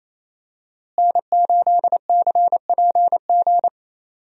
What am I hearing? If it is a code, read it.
Morse code, heard as a keyed tone, about 27 wpm, 700 Hz. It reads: N8CPG
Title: USA Call Signs - 5 Characters - Repeated 3x - 28wpm